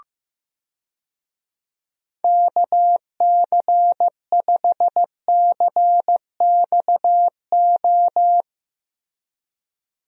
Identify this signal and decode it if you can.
Morse code, heard as a keyed tone, about 15 wpm, 700 Hz. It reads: KC5CXO
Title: USA Call Signs - 6 Characters - Repeated 2x - 15wpm